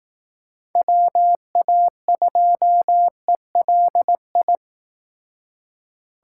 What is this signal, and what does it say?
Morse code, heard as a keyed tone, about 19 wpm, 700 Hz. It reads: WA2ELI